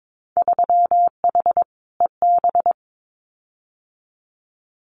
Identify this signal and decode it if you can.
Morse code, heard as a keyed tone, about 22 wpm, 700 Hz. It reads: 3H EB
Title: Winter Field Day Exchanges Encore 22wpm